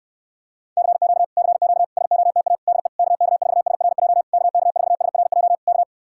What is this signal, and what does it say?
Morse code, heard as a keyed tone, about 59 wpm, 700 Hz. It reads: CQ CQ SOTA DE KK4SRJ KK4SRJ K